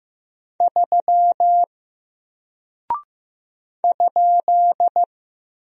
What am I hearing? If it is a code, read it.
Morse code, heard as a keyed tone, about 15 wpm, 700 Hz. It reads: 3 ?
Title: Single Character (Letter:Number:Punctuation:Prosign) Alt 15wpm